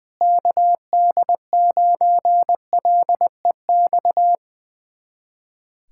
Morse code, heard as a keyed tone, about 21 wpm, 700 Hz. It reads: KD9LEX
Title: USA Callsign DE Callsign - 6 Characters - 20wpm